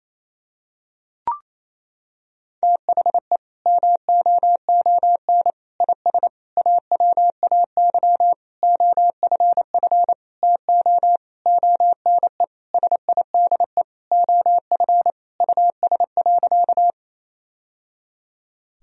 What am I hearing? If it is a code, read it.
Morse code, heard as a keyed tone, about 28 wpm, 700 Hz. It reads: THE MOON IS AWAY OFF TO ONE SIDE OF US.
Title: Sentences from Top 1000 Words - Review Part 3 - 28wpm